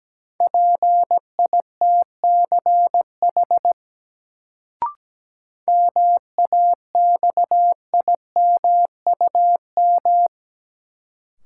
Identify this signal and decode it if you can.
Morse code, heard as a keyed tone, about 17 wpm, 700 Hz. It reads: PITCH MAXIMUM
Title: Sentences from Top 1300 Words - New Word Drill Alt - 17wpm